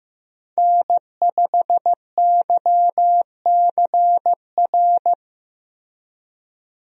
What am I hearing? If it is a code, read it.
Morse code, heard as a keyed tone, about 15 wpm, 700 Hz. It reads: N5YCR
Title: USA Call Signs - 5 Characters - Repeated 2x - 15wpm